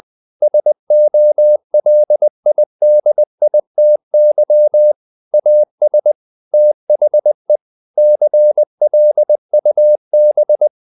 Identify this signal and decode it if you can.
Morse code, heard as a keyed tone, about 20 wpm, 600 Hz. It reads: SOLIDITY AS THE CLUB
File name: the old man and the sea part 2 20wpm